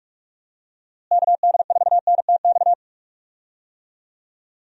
Morse code, heard as a keyed tone, about 46 wpm, 700 Hz. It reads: KD4NTX
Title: USA Call Signs - 6 Characters - Repeated 2x - 45wpm